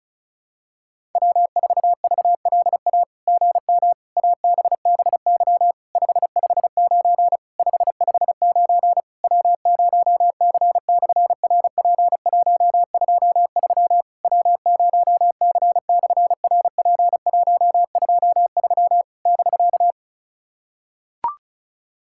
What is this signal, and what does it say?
Morse code, heard as a keyed tone, about 36 wpm, 700 Hz. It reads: W4VLA GM ABBY 559 559 W0C/RP123 W0C/RP123 <BK>
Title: SOTA QSO Practice 35wpm